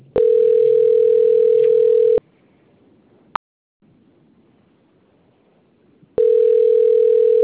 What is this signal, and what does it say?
Ringback tone (North America), 440 + 480 Hz.